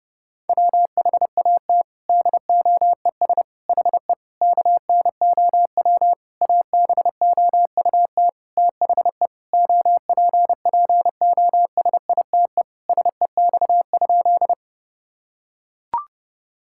Morse code, heard as a keyed tone, about 30 wpm, 700 Hz. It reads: WHAT DOES HE KNOW ABOUT THE OPPOSITE SEX?
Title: Sentences from Top 1600 Words 30wpm